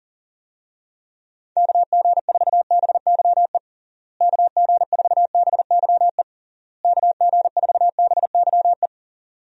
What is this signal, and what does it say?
Morse code, heard as a keyed tone, about 40 wpm, 700 Hz. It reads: KG4BYE KG4BYE KG4BYE
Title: USA Call Signs - 6 Characters - Repeated 3x - 40wpm